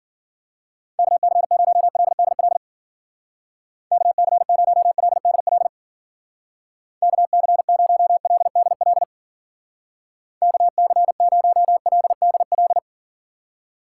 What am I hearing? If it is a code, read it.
Morse code, heard as a keyed tone, about 52 wpm, 690 Hz. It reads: KC0LDL KC0LDL KC0LDL KC0LDL